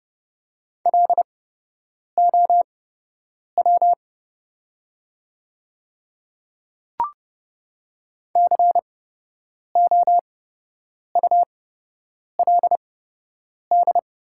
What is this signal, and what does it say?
Morse code, heard as a keyed tone, about 30 wpm, spaced as for 8 wpm, 690 Hz. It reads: LOW COULD
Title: 8x Character Spacing - Top 200 Words - 30wpm